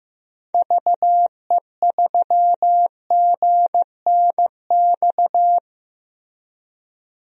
Morse code, heard as a keyed tone, about 15 wpm, 700 Hz. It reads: VE3GNX
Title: DXCC-100 Call Signs - 6 Characters - Repeated 2x - 15wpm